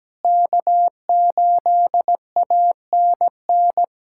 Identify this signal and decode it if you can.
Morse code, heard as a keyed tone, about 17 wpm, 700 Hz. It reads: K8ANN
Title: USA Callsign DE Callsign - 5 Characters - 17wpm